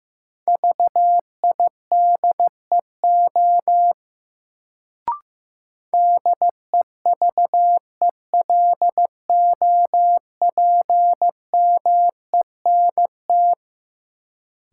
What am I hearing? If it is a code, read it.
Morse code, heard as a keyed tone, about 15 wpm, 700 Hz. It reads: VIDEO DEVELOPMENT
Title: Sentences from Top 1000 Words - Review Part 4 - New Word Drill - 15wpm